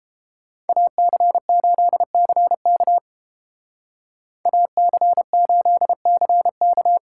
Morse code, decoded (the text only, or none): AC8CK AC8CK